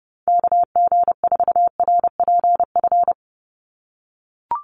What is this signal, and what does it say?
Morse code, heard as a keyed tone, about 30 wpm, 700 Hz. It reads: KG4RPF
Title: USA Call Signs 30wpm